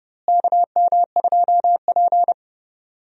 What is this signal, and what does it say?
Morse code, heard as a keyed tone, about 30 wpm, 700 Hz. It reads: KM2P
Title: USA Call Signs - 4 Characters - Repeated 3x - 30wpm